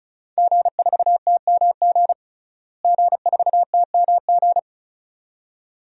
Morse code, heard as a keyed tone, about 34 wpm, 700 Hz. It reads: G4TMG G4TMG
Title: DXCC-100 Call Signs - 5 Characters - Repeated 2x - 35wpm